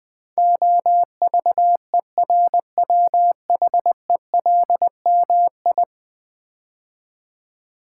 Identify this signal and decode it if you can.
Morse code, heard as a keyed tone, about 20 wpm, 700 Hz. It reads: OVERWHELMI
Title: Build A Word - 5 Letters At A Time - 12-Letter Words - 20wpm